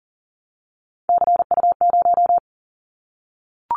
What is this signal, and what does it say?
Morse code, heard as a keyed tone, about 40 wpm, 700 Hz. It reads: CU0
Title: Callsign Prefixes for Top 50 DXCC - 40wpm